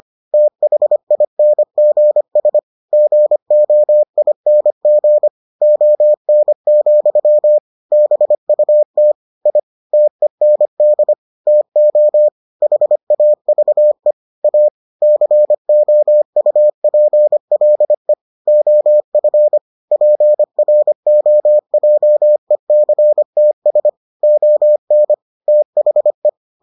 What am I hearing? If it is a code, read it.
Morse code, heard as a keyed tone, about 25 wpm, 600 Hz. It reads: THINGS GOING ON, BUT I TEND TO HAVE A COUPLE OF PROJECTS ON THE